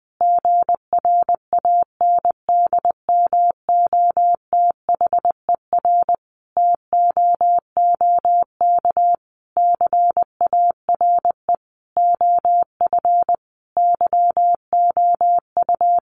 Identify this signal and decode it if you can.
Morse code, heard as a keyed tone, about 20 wpm, 700 Hz. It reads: GRANDMOTHER TOOK CARE OF YOU
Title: Sentences from Top 1200 Words 20wpm